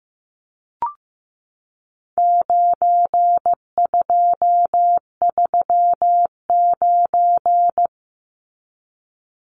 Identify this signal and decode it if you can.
Morse code, heard as a keyed tone, about 15 wpm, 700 Hz. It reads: 9239
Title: Numbers - 4-Digit 15wpm